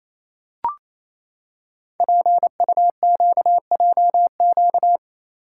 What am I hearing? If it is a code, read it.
Morse code, heard as a keyed tone, about 28 wpm, 700 Hz. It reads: PUQJQ
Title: Code Groups - 5 Letters - 28wpm